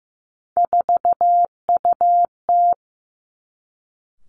Morse code, heard as a keyed tone, about 15 wpm, 700 Hz. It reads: 4UT
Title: Three Challenging Letter-Number Combinations 15wpm